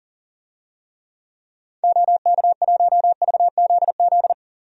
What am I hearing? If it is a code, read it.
Morse code, heard as a keyed tone, about 40 wpm, 700 Hz. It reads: OK1VZZ
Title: DXCC-100 Call Signs - 6 Characters - Repeated 2x - 40wpm